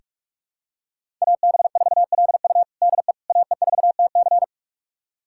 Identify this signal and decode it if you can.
Morse code, heard as a keyed tone, about 46 wpm, 700 Hz. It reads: AB4LU DE AE4TC
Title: USA Callsign DE Callsign - 5 Characters - 45wpm